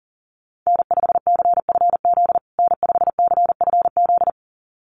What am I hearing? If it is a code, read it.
Morse code, heard as a keyed tone, about 40 wpm, 700 Hz. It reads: N5CFZ N5CFZ
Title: USA Call Signs Repeated 2x 40wpm